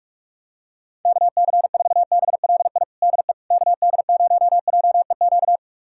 Morse code, heard as a keyed tone, about 46 wpm, 700 Hz. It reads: KC4BLI DE KD0JEQ
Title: USA Callsign DE Callsign 45wpm